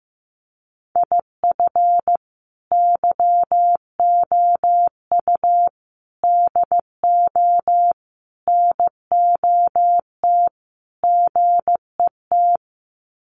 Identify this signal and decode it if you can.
Morse code, heard as a keyed tone, about 15 wpm, 700 Hz. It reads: IF YOU DO NOT GET
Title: Sets of 5 Words Encore2 15wpm